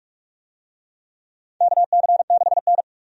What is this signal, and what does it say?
Morse code, heard as a keyed tone, about 46 wpm, 700 Hz. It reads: KC6N
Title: USA Call Signs - 4 Characters - 45wpm